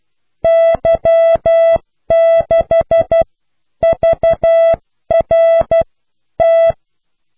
Morse code, heard as a keyed tone, about 12 wpm, spaced as for 10 wpm, 660 Hz. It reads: Y6VRT